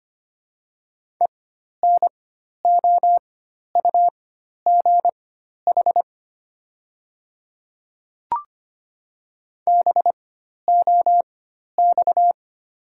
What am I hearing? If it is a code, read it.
Morse code, heard as a keyed tone, about 25 wpm, spaced as for 12 wpm, 700 Hz. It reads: ENOUGH BOX
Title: 4x Character Spacing - Top 500 Words - 25wpm